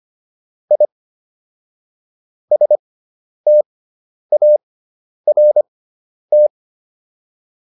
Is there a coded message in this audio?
Yes: Morse code from 0.7 to 6.5 s, heard as a keyed tone, about 25 wpm, spaced as for 10 wpm, 600 Hz. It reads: I START